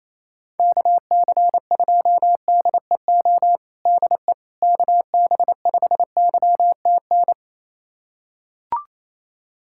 Morse code, heard as a keyed tone, about 28 wpm, 700 Hz. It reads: KC2DEO DE KB5YTN
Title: USA Callsign DE Callsign 28wpm